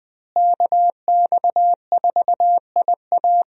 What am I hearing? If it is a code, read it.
Morse code, heard as a keyed tone, about 20 wpm, 700 Hz. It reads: KX4IA
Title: USA Call Signs - 5 Characters - Repeated 3x - 20wpm